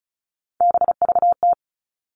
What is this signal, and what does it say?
Morse code, heard as a keyed tone, about 34 wpm, 700 Hz. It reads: BVT
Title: All Cut Numbers - 3-Digit 35wpm